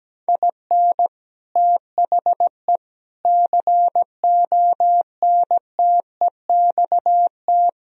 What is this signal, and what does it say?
Morse code, heard as a keyed tone, about 17 wpm, 700 Hz. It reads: IN THE CONTEXT
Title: Speed-Racing - Sets of 4 Words 17wpm